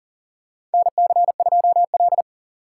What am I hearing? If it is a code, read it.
Morse code, heard as a keyed tone, about 39 wpm, 700 Hz. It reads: NC2L
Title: USA Call Signs - 4 Characters - 6x Character Spacing - 40wpm